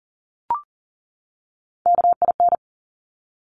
Morse code, heard as a keyed tone, about 42 wpm, 700 Hz. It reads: KIN
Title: Top Three Letter Combinations in Words 40wpm